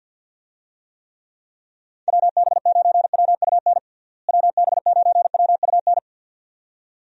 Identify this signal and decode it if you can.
Morse code, heard as a keyed tone, about 46 wpm, 700 Hz. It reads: WB9WUN WB9WUN
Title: USA Call Signs - 6 Characters - Repeated 2x - 50wpm